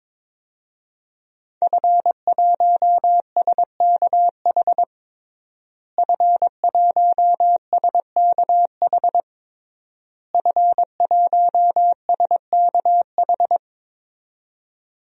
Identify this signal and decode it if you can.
Morse code, heard as a keyed tone, about 22 wpm, 700 Hz. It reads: F1SKH F1SKH F1SKH
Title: DXCC-100 Call Signs - 5 Characters - Repeated 3x - 22wpm